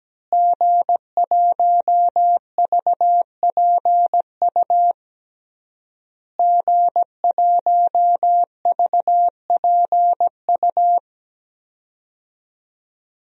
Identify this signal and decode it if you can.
Morse code, heard as a keyed tone, about 17 wpm, 700 Hz. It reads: G1VPU G1VPU
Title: DXCC-100 Call Signs - 5 Characters - Repeated 3x - 17wpm